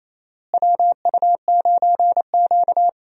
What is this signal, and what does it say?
Morse code, heard as a keyed tone, about 28 wpm, 700 Hz. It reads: WU9Q
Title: USA Call Signs - 4 Characters - Repeated 2x - 28wpm